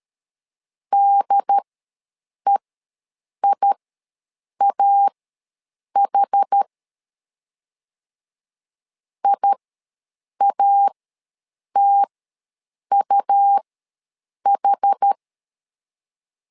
Morse code, heard as a keyed tone, about 13 wpm, spaced as for 7 wpm, 800 Hz. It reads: DEIAH IATUH
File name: LEZIONI / 2 UHDN / 2 UHD ws 12wpm 1722